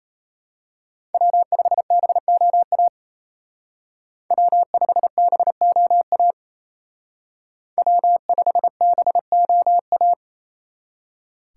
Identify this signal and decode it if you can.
Morse code, heard as a keyed tone, about 31 wpm, 700 Hz. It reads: W5BOA W5BOA W5BOA
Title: Speed Racing - USA 1x3 Call Signs 28wpm